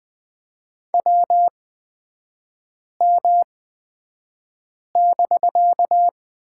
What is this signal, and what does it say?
Morse code, heard as a keyed tone, about 20 wpm, spaced as for 5 wpm, 700 Hz. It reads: WM<BK>
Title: 255 Lesson 41e - Period - Three Characters 20wpm